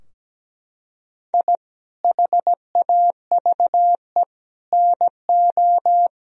Morse code, heard as a keyed tone, about 17 wpm, 700 Hz. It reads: I HAVE NO
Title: Sentences from Top 1000 Words - Review Part 4 - 17wpm